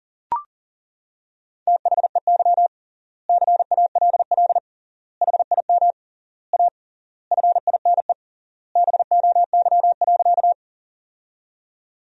Morse code, heard as a keyed tone, about 40 wpm, 700 Hz. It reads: THEY CALL HIM A FINE BOY.
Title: 3x Word-Spacing - Sentences from Top 500 Words - 40wpm